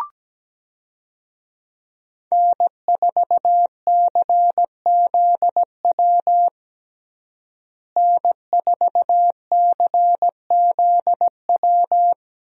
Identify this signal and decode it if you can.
Morse code, heard as a keyed tone, about 17 wpm, 700 Hz. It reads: N4CZW N4CZW
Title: USA Call Signs - 5 Characters - Repeated 2x - 17wpm